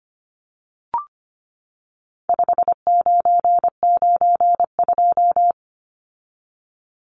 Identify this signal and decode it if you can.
Morse code, heard as a keyed tone, about 25 wpm, 700 Hz. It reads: 5992
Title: Numbers - 4-Digit 25wpm